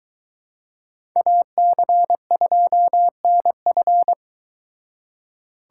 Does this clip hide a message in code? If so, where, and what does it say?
Yes: Morse code from 1.2 to 4.1 s, heard as a keyed tone, about 23 wpm, 700 Hz. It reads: AC2NF